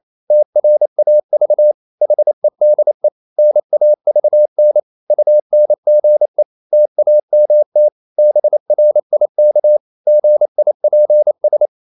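Morse code, heard as a keyed tone, about 28 wpm, 600 Hz. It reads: TRAV HEDE NAVN UNGE TAMT BRIK GIPS
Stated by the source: Danish short words sent in Morse at 28 wpm, from CW med Gnister.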